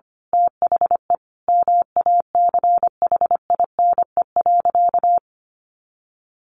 Morse code, heard as a keyed tone, about 25 wpm, 700 Hz. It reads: THE MACHINE.
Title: A Sound Of Thunder 25wpm